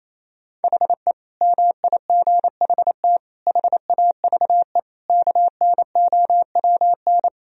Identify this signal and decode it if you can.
Morse code, heard as a keyed tone, about 28 wpm, 700 Hz. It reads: HE MIGHT HAVE KNOWN
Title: Sentences from Top 1000 Words - Review Part 3 - 28wpm